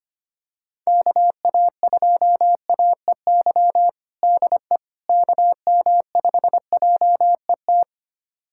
Morse code, heard as a keyed tone, about 25 wpm, 700 Hz. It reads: KA2AEY DE KM5JET
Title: USA Callsign DE Callsign - 6 Characters - 25wpm